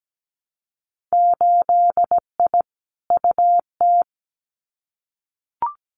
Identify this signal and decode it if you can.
Morse code, heard as a keyed tone, about 17 wpm, 700 Hz. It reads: 8I UT
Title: Winter Field Day Exchanges Encore 17wpm